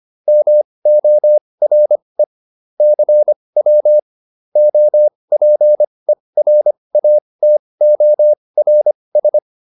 Morse code, heard as a keyed tone, about 25 wpm, spaced as for 20 wpm, 600 Hz. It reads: MORE CW OPERATORS
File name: SS206 20